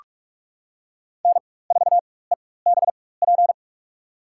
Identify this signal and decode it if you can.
Morse code, heard as a keyed tone, about 46 wpm, spaced as for 21 wpm, 700 Hz. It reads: N4EBP
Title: USA Call Signs - 4x Character Spacing - 45wpm